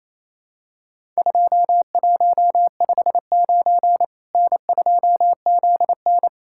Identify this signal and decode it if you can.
Morse code, heard as a keyed tone, about 28 wpm, 700 Hz. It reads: 2159 N2ZN
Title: Worked All Europe DX Contest - QTC Exchange - 28wpm